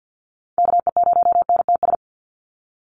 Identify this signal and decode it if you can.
Morse code, heard as a keyed tone, about 52 wpm, 700 Hz. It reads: KE0NTS
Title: USA Call Signs 50wpm